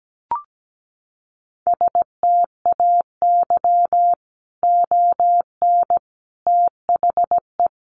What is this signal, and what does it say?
Morse code, heard as a keyed tone, about 17 wpm, 700 Hz. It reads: STAY ON THE
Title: A Sound Of Thunder 17wpm